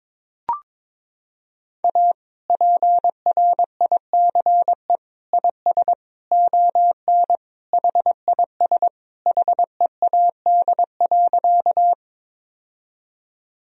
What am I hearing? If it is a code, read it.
Morse code, heard as a keyed tone, about 22 wpm, 700 Hz. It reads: A PRICE IS ON HIS HEAD.
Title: Sentences from Top 1000 Words - Review Part 4 - 22wpm